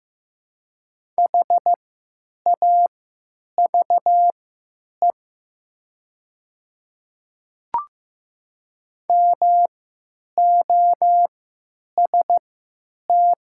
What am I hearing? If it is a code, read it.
Morse code, heard as a keyed tone, about 15 wpm, spaced as for 9 wpm, 700 Hz. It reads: HAVE MOST
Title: 3x Character Spacing - Top 100 Words - 15wpm